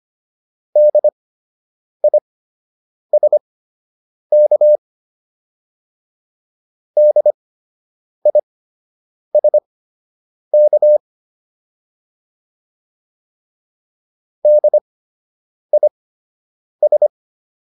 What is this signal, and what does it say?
Morse code, heard as a keyed tone, about 25 wpm, spaced as for 8 wpm, 600 Hz. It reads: DISK DISK DIS